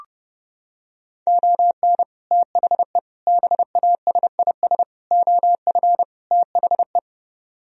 Morse code, heard as a keyed tone, about 31 wpm, 700 Hz. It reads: ON THE BASIS OF THE